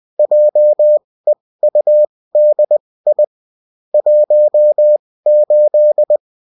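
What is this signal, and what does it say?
Morse code, heard as a keyed tone, about 20 wpm, spaced as for 16 wpm, 600 Hz. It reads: JEUDI 18
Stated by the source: Bulletin in text and CW form.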